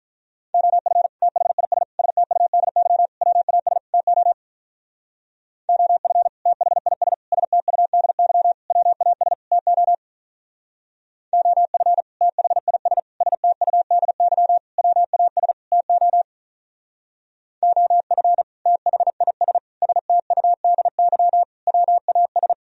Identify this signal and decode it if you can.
Morse code, heard as a keyed tone, about 43 wpm, 700 Hz. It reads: OF THIS STUDY WAS TO OF THIS STUDY WAS TO OF THIS STUDY WAS TO OF THIS STUDY WAS
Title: Speed-Racing - Sets of 5 Words 35wpm